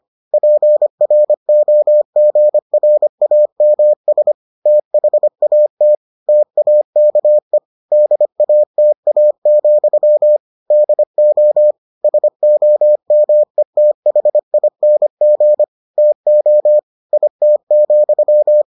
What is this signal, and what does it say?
Morse code, heard as a keyed tone, about 25 wpm, 600 Hz. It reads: PROGRAMS THAT TAKE DATA, DO SOMETHING TO IT,